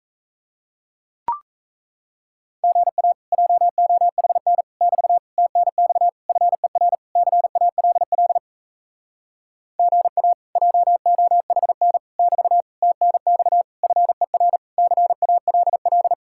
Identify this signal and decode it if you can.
Morse code, heard as a keyed tone, about 37 wpm, 700 Hz. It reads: GA JOHN <BT> TNX FER CALL GA JOHN <BT> TNX FER CALL